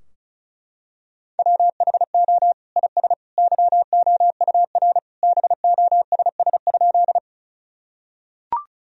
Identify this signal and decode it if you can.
Morse code, heard as a keyed tone, about 36 wpm, 700 Hz. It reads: WHO IS YOUR BOSS?
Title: Sentences from Top 1000 Words - Review Part 4 - 35wpm